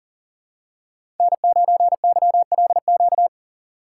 Morse code, heard as a keyed tone, about 40 wpm, 700 Hz. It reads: N9YLQ
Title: USA Call Signs - 5 Characters - 6x Character Spacing - 40wpm